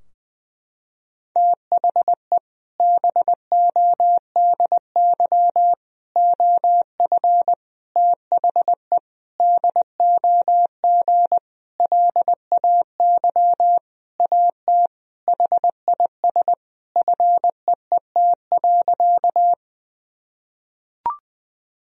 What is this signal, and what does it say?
Morse code, heard as a keyed tone, about 19 wpm, 700 Hz. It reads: THE BODY OF THE DOG LAY AT HIS FEET.
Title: Sentences from Top 1000 Words - Review Part 2 - 20wpm